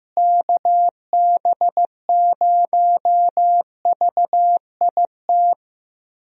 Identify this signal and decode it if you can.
Morse code, heard as a keyed tone, about 15 wpm, 700 Hz. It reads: KB0VIT
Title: USA Call Signs - 6 Characters - Repeated 3x - 15wpm